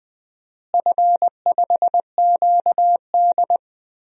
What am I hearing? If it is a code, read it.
Morse code, heard as a keyed tone, about 20 wpm, 700 Hz. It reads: F5QD
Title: DXCC-100 Call Signs - 4 Characters - Repeated 2x - 20wpm